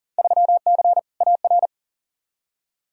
Morse code, heard as a keyed tone, about 40 wpm, 700 Hz. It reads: 3C AR
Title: Field Day Exchanges Encore - 40wpm